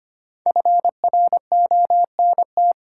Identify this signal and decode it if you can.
Morse code, heard as a keyed tone, about 25 wpm, 700 Hz. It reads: FRONT
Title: Sentences from Top 500 Words - New Word Drill - 25wpm